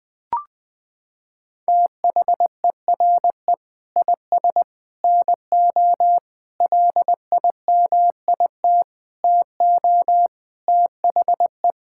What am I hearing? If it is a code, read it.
Morse code, heard as a keyed tone, about 21 wpm, 700 Hz. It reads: THERE IS NO LIMIT TO THE
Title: Sentences with 8-Word Phrases - 20wpm